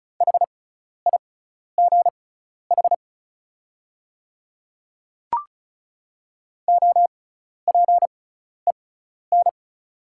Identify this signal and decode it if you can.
Morse code, heard as a keyed tone, about 36 wpm, spaced as for 12 wpm, 690 Hz. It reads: HIGH OPEN
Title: 6x Character Spacing - Top 300 Words - 35wpm